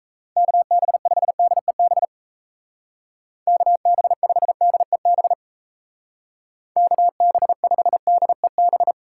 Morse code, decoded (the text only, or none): KB5DEB KB5DEB KB5DEB